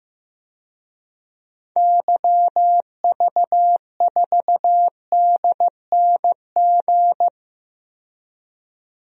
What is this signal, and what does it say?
Morse code, heard as a keyed tone, about 15 wpm, 700 Hz. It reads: YV4DNG
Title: DXCC-100 Call Signs - 6 Characters - 15wpm